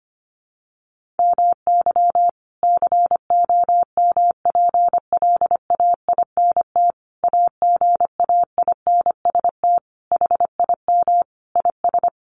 Morse code, heard as a keyed tone, about 25 wpm, 700 Hz. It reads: MY COMPLAINT AGAINST HIM IS
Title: Sentences from Top 1400 Words 25wpm